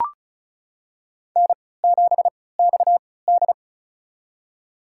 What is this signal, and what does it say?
Morse code, heard as a keyed tone, about 36 wpm, spaced as for 20 wpm, 700 Hz. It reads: N7XD
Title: USA Call Signs - 4 Characters - 3x Character Spacing - 35wpm